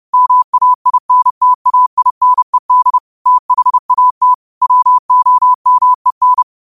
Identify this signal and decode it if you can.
Morse code, heard as a keyed tone, about 30 wpm, 1000 Hz. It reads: MAINTAINED THAT WOMEN